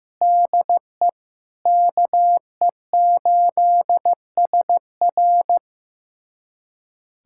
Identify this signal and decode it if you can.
Morse code, heard as a keyed tone, about 15 wpm, 700 Hz. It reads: DE KE8SR
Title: USA Callsign DE Callsign - 5 Characters - 15wpm